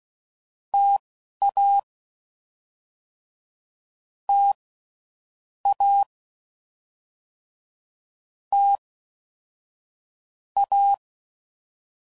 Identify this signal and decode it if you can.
Morse code, heard as a keyed tone, about 16 wpm, spaced as for 11 wpm, 800 Hz. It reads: TA T A T A
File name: Combinations / Difficult Combos / T AOEI's